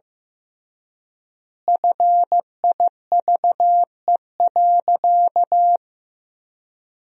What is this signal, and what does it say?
Morse code, heard as a keyed tone, about 15 wpm, 700 Hz. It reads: FIVE.
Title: 3x Word-Spacing - Sentences from Top 400 Words - 15wpm